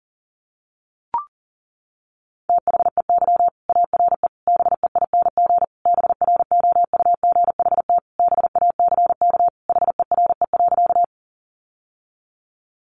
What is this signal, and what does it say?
Morse code, heard as a keyed tone, about 40 wpm, 700 Hz. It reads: THEY ARE BEING BROUGHT BACK HERE.
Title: Sentences from Top 600 Words 40wpm